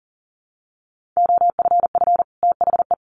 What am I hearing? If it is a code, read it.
Morse code, heard as a keyed tone, about 40 wpm, 700 Hz. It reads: OFF THE
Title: Sets of 2 Words 40wpm